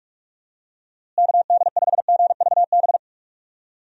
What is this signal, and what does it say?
Morse code, heard as a keyed tone, about 47 wpm, 700 Hz. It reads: KD5GVB
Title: USA Call Signs - 6 Characters - Repeated 2x - 45wpm